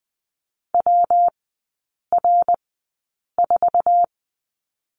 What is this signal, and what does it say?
Morse code, heard as a keyed tone, about 20 wpm, spaced as for 8 wpm, 700 Hz. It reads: WR4
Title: 138 Lesson 23e - Letter f - Three Characters 20wpm